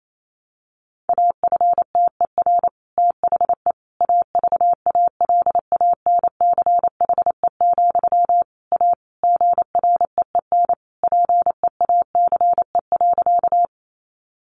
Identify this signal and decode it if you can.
Morse code, heard as a keyed tone, about 28 wpm, 700 Hz. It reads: AFTER THE AVALANCHE, A GREEN PEACE.
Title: A Sound Of Thunder 28wpm